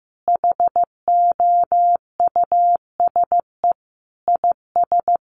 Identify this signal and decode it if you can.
Morse code, heard as a keyed tone, about 15 wpm, 700 Hz. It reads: HOUSE IS
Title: Sentences from Top 1000 Words 15wpm